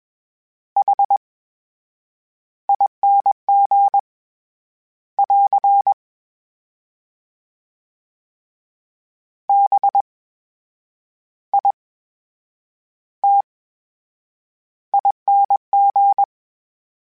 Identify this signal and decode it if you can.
Morse code, heard as a keyed tone, about 21 wpm, spaced as for 8 wpm, 800 Hz. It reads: HING <AR> BITING
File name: Special INGs-mp3s / Special INGs-mp3s / VerySlowToFaster Ing @20